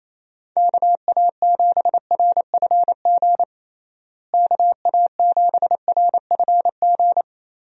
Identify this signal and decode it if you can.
Morse code, heard as a keyed tone, about 28 wpm, 700 Hz. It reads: KA7RFG KA7RFG
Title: USA Call Signs - 6 Characters - Repeated 2x - 28wpm